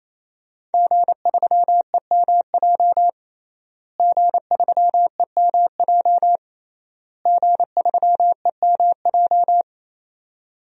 Morse code, heard as a keyed tone, about 28 wpm, 700 Hz. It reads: G3EMJ G3EMJ G3EMJ
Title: DXCC-100 Call Signs - 5 Characters - Repeated 3x - 28wpm